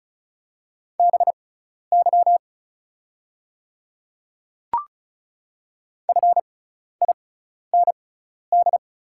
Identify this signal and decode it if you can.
Morse code, heard as a keyed tone, about 36 wpm, spaced as for 12 wpm, 690 Hz. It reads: BY FIND
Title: 6x Character Spacing - Top 200 Words - 35wpm